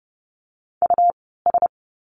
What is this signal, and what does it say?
Morse code, heard as a keyed tone, about 30 wpm, spaced as for 17 wpm, 700 Hz. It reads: US
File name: course / 30:10wpm / 230 Lesson 37f - Letter x - Words 30wpm